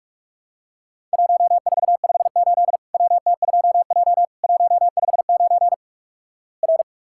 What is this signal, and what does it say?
Morse code, heard as a keyed tone, about 46 wpm, 700 Hz. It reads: 1458 WT2J 159 R
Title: Worked All Europe DX Contest - QTC Exchange - 45wpm